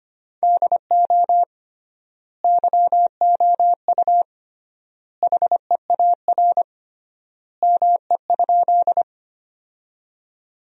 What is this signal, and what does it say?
Morse code, heard as a keyed tone, about 25 wpm, 700 Hz. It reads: DO YOU HEAR ME?
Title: 3x Word-Spacing - Sentences from Top 300 Words - 25wpm